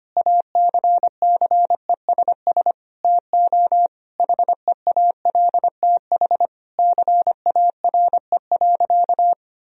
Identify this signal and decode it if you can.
Morse code, heard as a keyed tone, about 25 wpm, 700 Hz. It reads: ACCESS TO HEALTH CARE.
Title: Sentences from Top 1000 Words - Review Part 4 - 25wpm